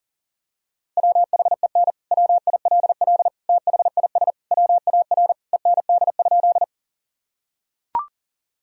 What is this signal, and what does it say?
Morse code, heard as a keyed tone, about 40 wpm, 700 Hz. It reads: WHEN WILL THIS WAR END?
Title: Sentences from Top 1000 Words - Review Part 2 - 40wpm